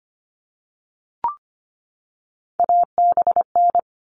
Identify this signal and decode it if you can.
Morse code, heard as a keyed tone, about 25 wpm, 700 Hz. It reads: ABN
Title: All Cut Numbers - 3-Digit 25wpm